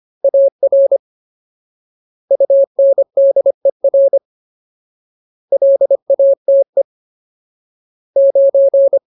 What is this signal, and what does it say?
Morse code, heard as a keyed tone, about 25 wpm, 550 Hz. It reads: AR UNDER LATE 9
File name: Ultimate Word List / 25WPM Ultimate